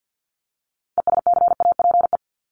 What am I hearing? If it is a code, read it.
Morse code, heard as a keyed tone, about 50 wpm, 700 Hz. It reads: ESCAPE
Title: Top 1501-1750 Words 50wpm